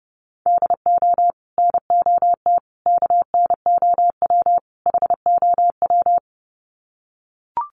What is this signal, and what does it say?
Morse code, heard as a keyed tone, about 30 wpm, 700 Hz. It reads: DO NOT KNOW HOW
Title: Sets of 4 Words 30wpm